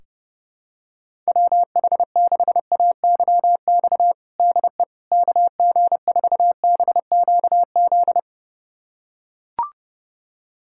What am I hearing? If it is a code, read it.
Morse code, heard as a keyed tone, about 30 wpm, 700 Hz. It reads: WH6AYX DE KG4BQZ
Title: USA Callsign DE Callsign 30wpm